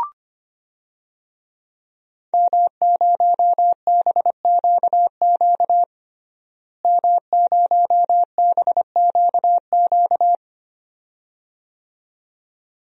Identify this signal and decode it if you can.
Morse code, heard as a keyed tone, about 25 wpm, 700 Hz. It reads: M0BQQ M0BQQ
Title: DXCC-100 Call Signs - 5 Characters - Repeated 2x - 25wpm